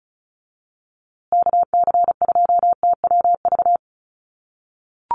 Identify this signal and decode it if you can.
Morse code, heard as a keyed tone, about 34 wpm, 700 Hz. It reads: KC2TWV
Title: USA Call Signs Repeated 3x 35wpm